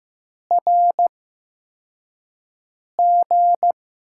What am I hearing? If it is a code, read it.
Morse code, heard as a keyed tone, about 15 wpm, spaced as for 4 wpm, 700 Hz. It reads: RG
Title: USA Call Signs - 6 Characters - 8x Character Spacing - 15wpm